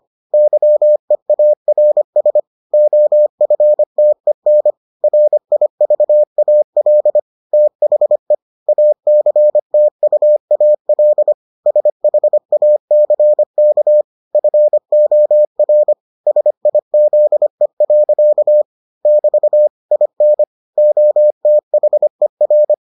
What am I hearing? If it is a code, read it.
Morse code, heard as a keyed tone, about 25 wpm, 600 Hz. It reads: YEARS OFTEN RIVAL THE ACTUAL SHACK FOR SIZE. <BT> IN OTHER